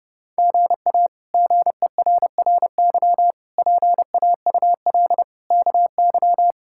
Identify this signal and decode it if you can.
Morse code, heard as a keyed tone, about 30 wpm, 700 Hz. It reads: GA GERRY PAUL KY
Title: K1USN SST QSO Practice - 30wpm